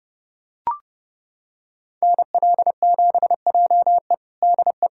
Morse code, heard as a keyed tone, about 30 wpm, 700 Hz. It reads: NL7JE DE
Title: USA Callsign DE Callsign 30wpm